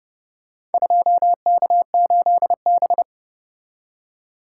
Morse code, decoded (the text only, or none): 2K8B